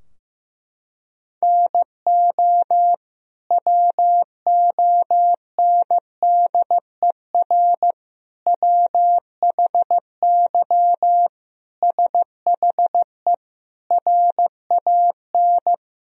Morse code, heard as a keyed tone, about 15 wpm, 700 Hz. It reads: NO WONDER WHY SHE RAN
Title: Sentences from Top 1000 Words - Review Part 3 - 15wpm